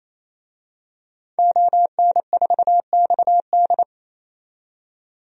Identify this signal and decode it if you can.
Morse code, heard as a keyed tone, about 28 wpm, 700 Hz. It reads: ON4XD
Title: DXCC-100 Call Signs - 5 Characters - 28wpm